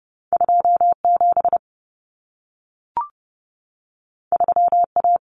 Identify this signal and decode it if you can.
Morse code, heard as a keyed tone, about 30 wpm, 700 Hz. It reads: 27 3A
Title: Common Cut Numbers - 2-Digit 30wpm